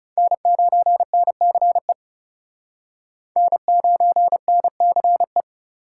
Morse code, heard as a keyed tone, about 37 wpm, 700 Hz. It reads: N9NCE N9NCE
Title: Speed Racing - USA 1x3 Call Signs 30wpm